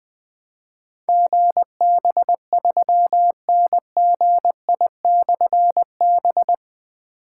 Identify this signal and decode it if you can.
Morse code, heard as a keyed tone, about 20 wpm, 700 Hz. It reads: GB3NGI/B
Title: Difficult Call Signs 20wpm